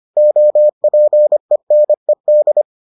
Morse code, heard as a keyed tone, about 25 wpm, 600 Hz. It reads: OPENED